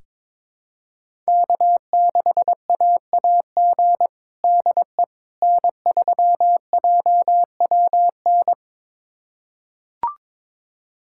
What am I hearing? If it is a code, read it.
Morse code, heard as a keyed tone, about 22 wpm, 700 Hz. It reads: K6AAG DE N3JWN
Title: USA Callsign DE Callsign - 5 Characters - 22wpm